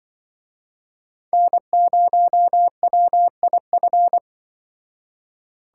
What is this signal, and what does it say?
Morse code, heard as a keyed tone, about 25 wpm, 700 Hz. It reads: N0WIF
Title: Speed Racing - USA Call Signs 20wpm